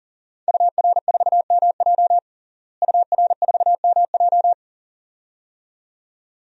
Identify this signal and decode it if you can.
Morse code, heard as a keyed tone, about 37 wpm, 700 Hz. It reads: UR4MJ UR4MJ
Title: DXCC-100 Call Signs - 5 Characters - Repeated 2x - 40wpm